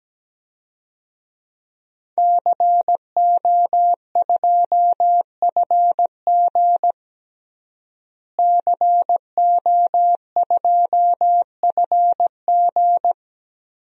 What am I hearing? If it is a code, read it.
Morse code, heard as a keyed tone, about 17 wpm, 700 Hz. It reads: CO2FG CO2FG
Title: DXCC-100 Call Signs - 5 Characters - Repeated 2x - 17wpm